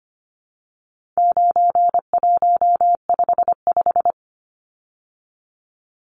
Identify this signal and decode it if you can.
Morse code, heard as a keyed tone, about 25 wpm, 700 Hz. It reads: 9155
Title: Numbers - 4-Digit 25wpm